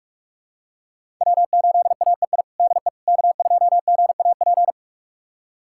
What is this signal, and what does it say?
Morse code, heard as a keyed tone, about 46 wpm, 700 Hz. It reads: W8AEI DE K2GAP
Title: USA Callsign DE Callsign - 5 Characters - 45wpm